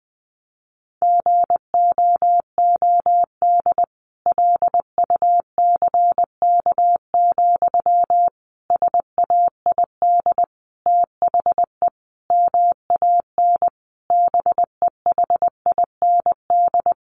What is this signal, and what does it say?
Morse code, heard as a keyed tone, about 20 wpm, 700 Hz. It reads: GOOD LUCK, SAID THE MAN BEHIND
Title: A Sound Of Thunder 20wpm